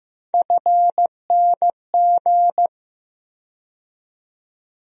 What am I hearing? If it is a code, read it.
Morse code, heard as a keyed tone, about 15 wpm, 700 Hz. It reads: FNG
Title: USA Call Sign Postfixes - 3 Letters 15wpm